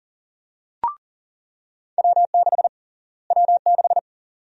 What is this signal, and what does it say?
Morse code, heard as a keyed tone, about 40 wpm, 700 Hz. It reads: W6 W6
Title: USA Call Sign Prefixes - Repeated 2x - 40wpm